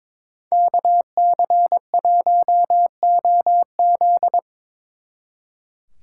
Morse code, heard as a keyed tone, about 25 wpm, 700 Hz. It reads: KC1OZ
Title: Speed Racing - USA 2x2 Call Signs 22wpm